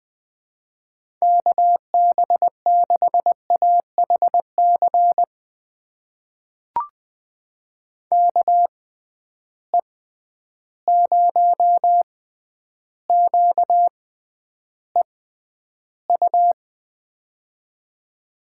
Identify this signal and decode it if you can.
Morse code, heard as a keyed tone, about 20 wpm, spaced as for 10 wpm, 700 Hz. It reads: KB6AHC KE0QEU
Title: USA Call Signs - 6 Characters - 6x Character Spacing - 20wpm